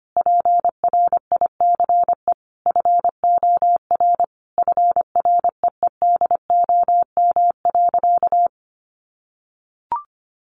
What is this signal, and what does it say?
Morse code, heard as a keyed tone, about 25 wpm, 700 Hz. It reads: PRICE FOR FREEDOM.
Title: Sentences from Top 1100 Words 25wpm